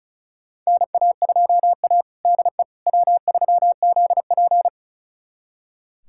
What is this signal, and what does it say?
Morse code, heard as a keyed tone, about 36 wpm, 700 Hz. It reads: NA2A DE W3ZP
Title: USA Callsign DE Callsign - 4 Characters - 35wpm